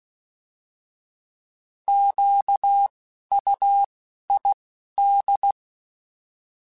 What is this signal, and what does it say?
Morse code, heard as a keyed tone, about 16 wpm, 800 Hz. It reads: Q U I D
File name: Alphabetical / Alphabetical mp3s / Qs @10